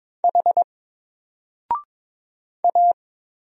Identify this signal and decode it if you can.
Morse code, heard as a keyed tone, about 22 wpm, 700 Hz. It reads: H A
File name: Single-Letter-Number-22wpm